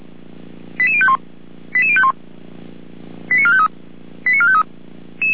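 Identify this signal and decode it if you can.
Five-tone selective call (ZVEI), heard as quick runs of five tones, tones of about 70 ms. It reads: 88051 88051 89453 89453
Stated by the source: ZVEI five-tone call received on SDR.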